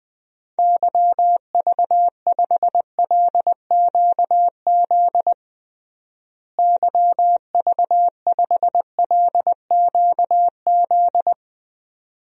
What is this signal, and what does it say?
Morse code, heard as a keyed tone, about 21 wpm, 700 Hz. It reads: YV5LQZ YV5LQZ
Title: DXCC-100 Call Signs - 6 Characters - Repeated 2x - 20wpm